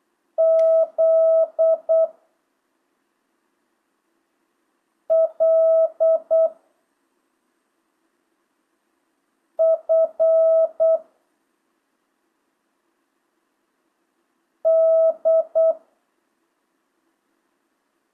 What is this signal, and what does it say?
Morse code, heard as a keyed tone, about 8 wpm, 650 Hz. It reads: Z L F D
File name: jh8cbh / 2025cw